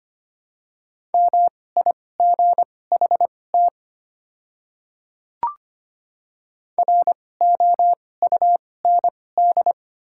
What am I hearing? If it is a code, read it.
Morse code, heard as a keyed tone, about 25 wpm, spaced as for 18 wpm, 700 Hz. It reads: MIGHT ROUND
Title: 2x Character Spacing - Top 300 Words - 25wpm